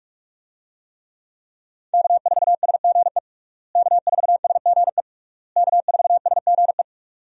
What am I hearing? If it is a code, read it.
Morse code, heard as a keyed tone, about 47 wpm, 700 Hz. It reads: K4SGE K4SGE K4SGE
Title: USA Call Signs - 5 Characters - Repeated 3x - 45wpm